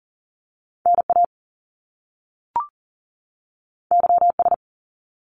Morse code, heard as a keyed tone, about 38 wpm, 700 Hz. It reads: NA YS
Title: Top Two Letter Combinations in Words 40wpm